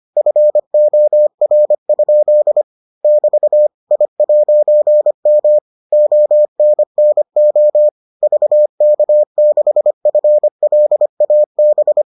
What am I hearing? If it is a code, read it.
Morse code, heard as a keyed tone, about 25 wpm, 600 Hz. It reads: FOR? <BT> I'M ONNO VK6FLAB